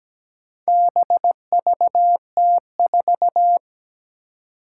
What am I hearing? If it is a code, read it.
Morse code, heard as a keyed tone, about 17 wpm, 700 Hz. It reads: BVT4
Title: Four Challenging Letter-Number Combinations 17wpm